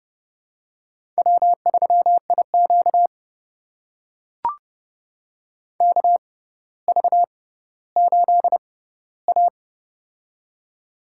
Morse code, heard as a keyed tone, about 30 wpm, spaced as for 15 wpm, 700 Hz. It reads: W3IQ KV8A
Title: USA Call Signs - 4 Characters - 6x Character Spacing - 30wpm